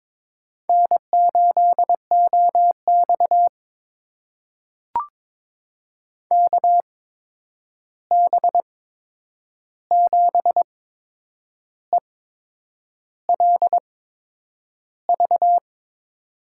Morse code, heard as a keyed tone, about 22 wpm, spaced as for 8 wpm, 700 Hz. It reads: N8OX KB7ELV
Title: USA Call Signs - 8x Character Spacing - 22wpm